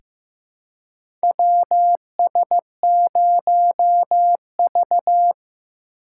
Morse code, heard as a keyed tone, about 15 wpm, 700 Hz. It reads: WS0V
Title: USA Call Signs - 3x Character Spacing - 15wpm